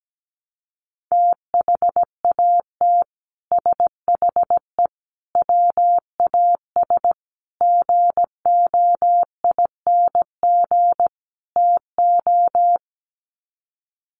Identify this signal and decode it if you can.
Morse code, heard as a keyed tone, about 17 wpm, 700 Hz. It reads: THAT SHE WAS GOING TO
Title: Sets of 5 Words Encore2 17wpm